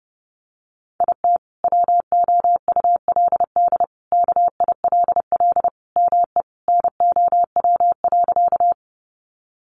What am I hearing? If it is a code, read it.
Morse code, heard as a keyed tone, about 30 wpm, 700 Hz. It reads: IT WOULD KILL ME NOW.
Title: Sentences from Top 1400 Words 30wpm